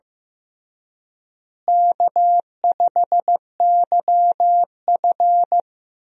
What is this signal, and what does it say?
Morse code, heard as a keyed tone, about 16 wpm, 700 Hz. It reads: K5YF
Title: USA Call Signs - 4 Characters - Repeated 2x - 15wpm